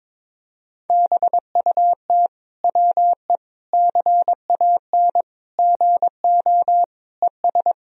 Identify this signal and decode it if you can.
Morse code, heard as a keyed tone, about 22 wpm, 700 Hz. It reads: BUT WE CAN GO ES
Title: Sentences from Top 100 Words and Abbreviations 22wpm